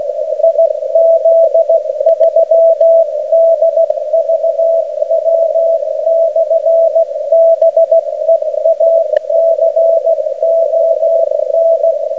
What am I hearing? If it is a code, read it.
Morse code, heard as a keyed tone, about 16 wpm, 660 Hz. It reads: IZ3DVW/BEACON
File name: Beacon recordings / IZ3DVW 7